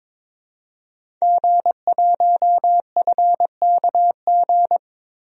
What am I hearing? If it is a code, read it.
Morse code, heard as a keyed tone, about 22 wpm, 700 Hz. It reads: G1FKG
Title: DXCC-100 Call Signs - 5 Characters - Repeated 2x - 22wpm